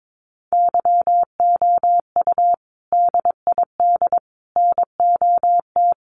Morse code, heard as a keyed tone, about 22 wpm, 700 Hz. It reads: YOU DID NOT
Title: Sentences from Top 1100 Words 22wpm